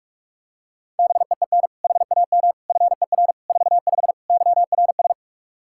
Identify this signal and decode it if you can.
Morse code, heard as a keyed tone, about 46 wpm, 700 Hz. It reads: BEEN HAM FER 45 YRS
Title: Practice QSO - Exchange 3 - More Info - 45wpm